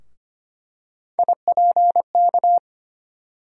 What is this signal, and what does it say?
Morse code, heard as a keyed tone, about 25 wpm, 700 Hz. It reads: IPK
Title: USA Call Sign Postfixes - 3 Letters 25wpm